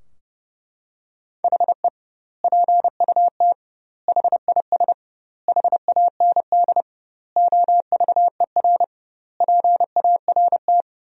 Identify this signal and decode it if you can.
Morse code, heard as a keyed tone, about 30 wpm, 700 Hz. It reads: HE PUT HIS HAND OVER PART
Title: 2x Word-Spacing - Sentences from Top 400 Words - 30wpm